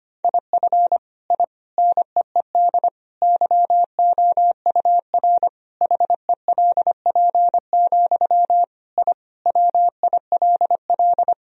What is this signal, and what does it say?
Morse code, heard as a keyed tone, about 25 wpm, 700 Hz. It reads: IF I NEED YOUR HELP, I WILL
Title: Sentences from Top 1000 Words - Review Part 1 - 25wpm